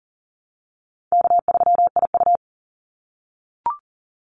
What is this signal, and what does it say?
Morse code, heard as a keyed tone, about 40 wpm, 700 Hz. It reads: K3IU
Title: Easy Call Signs - 40wpm